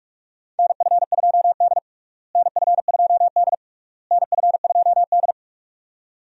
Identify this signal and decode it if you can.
Morse code, heard as a keyed tone, about 46 wpm, 700 Hz. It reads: NF2D NF2D NF2D
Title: USA Call Signs - 4 Characters - Repeated 3x - 45wpm